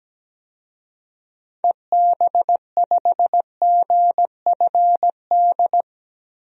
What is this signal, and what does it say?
Morse code, heard as a keyed tone, about 17 wpm, 700 Hz. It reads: EB5GFD
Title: DXCC-100 Call Signs - 6 Characters - 17wpm